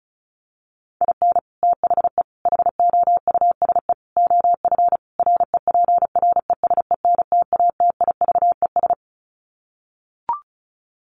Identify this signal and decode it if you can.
Morse code, heard as a keyed tone, about 34 wpm, 700 Hz. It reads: IN THE HOUSE OF REPRESENTATIVES
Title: Sets of 5 Words Encore 35wpm